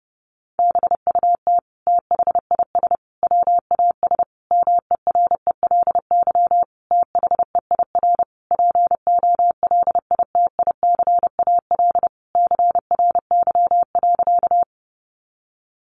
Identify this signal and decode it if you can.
Morse code, heard as a keyed tone, about 30 wpm, 700 Hz. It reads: BUT THIS WAS MERELY THEIR POLITICAL CRY.
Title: Sentences from Top 1400 Words 30wpm